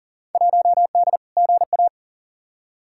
Morse code, heard as a keyed tone, about 40 wpm, 700 Hz. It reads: 1D GA
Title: Field Day Exchanges - 40wpm